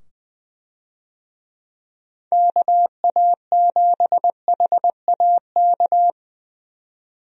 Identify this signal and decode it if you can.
Morse code, heard as a keyed tone, about 21 wpm, 700 Hz. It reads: KA7HAK
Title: USA Call Signs - 6 Characters - 20wpm